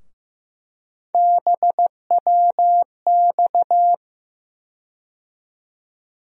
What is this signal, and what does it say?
Morse code, heard as a keyed tone, about 15 wpm, 700 Hz. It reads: BWX
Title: USA Call Sign Postfixes - 3 Letters 15wpm